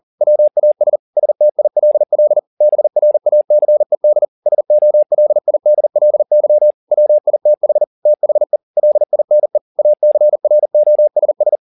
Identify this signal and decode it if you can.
Morse code, heard as a keyed tone, about 40 wpm, 600 Hz. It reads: WAS STILL BRACED SOLIDLY WITH THE LINE ACROSS